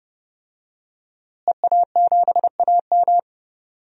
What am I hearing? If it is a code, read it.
Morse code, heard as a keyed tone, about 30 wpm, 700 Hz. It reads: EA7AM
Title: DXCC-100 Call Signs - 5 Characters - Repeated 2x - 30wpm